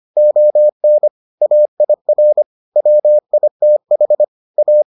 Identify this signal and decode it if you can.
Morse code, heard as a keyed tone, about 25 wpm, 600 Hz. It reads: ON AIR WITH A